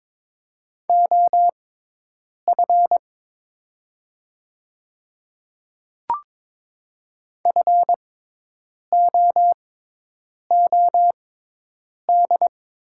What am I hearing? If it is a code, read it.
Morse code, heard as a keyed tone, about 22 wpm, spaced as for 8 wpm, 690 Hz. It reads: OF FOOD
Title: 6x Character Spacing - Top 300 Words - 22wpm